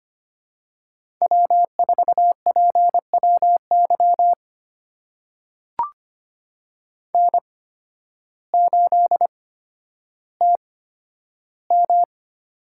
Morse code, heard as a keyed tone, about 25 wpm, spaced as for 12 wpm, 700 Hz. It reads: W4PWY N8TM
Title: USA Call Signs - 8x Character Spacing - 25wpm